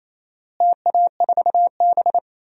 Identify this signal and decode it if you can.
Morse code, heard as a keyed tone, about 28 wpm, 700 Hz. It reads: TA4B
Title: DXCC-100 Call Signs - 4 Characters - Repeated 2x - 28wpm